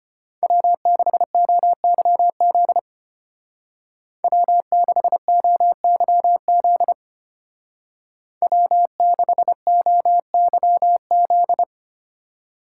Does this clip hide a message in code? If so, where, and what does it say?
Yes: Morse code from 0.4 to 11.6 s, heard as a keyed tone, about 33 wpm, 700 Hz. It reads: W6OYZ W6OYZ W6OYZ